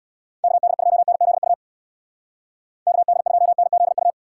Morse code, heard as a keyed tone, about 64 wpm, 700 Hz. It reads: KD2NZU KD2NZU